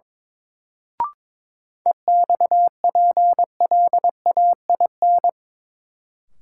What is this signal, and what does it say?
Morse code, heard as a keyed tone, about 22 wpm, 700 Hz. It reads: EXPLAIN
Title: Sentences from Top 1100 Words - New Word Drill Alt - 22wpm